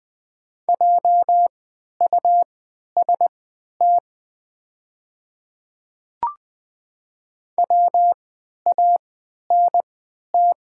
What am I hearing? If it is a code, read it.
Morse code, heard as a keyed tone, about 20 wpm, spaced as for 11 wpm, 700 Hz. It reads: JUST WANT
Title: 3x Character Spacing - Top 200 Words - 20wpm